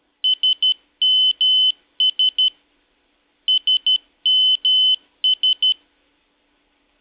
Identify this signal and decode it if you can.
Morse code, heard as a keyed tone, about 12 wpm, 3050 Hz. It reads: SMS SMS
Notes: Sonido de mensaje entrante sms en morse